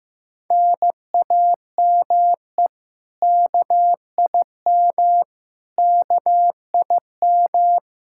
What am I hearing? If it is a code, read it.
Morse code, heard as a keyed tone, about 16 wpm, 700 Hz. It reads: NAME KIM KIM
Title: K1USN SST QSO Practice 15wpm